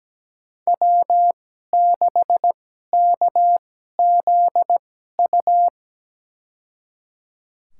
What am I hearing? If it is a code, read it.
Morse code, heard as a keyed tone, about 17 wpm, spaced as for 12 wpm, 700 Hz. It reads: W6KZU
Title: USA Call Signs - 5 Characters - 2x Character Spacing - 17wpm